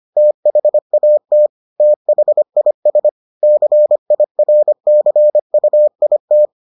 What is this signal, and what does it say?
Morse code, heard as a keyed tone, about 25 wpm, 600 Hz. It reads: THAT THIS CIRCUIT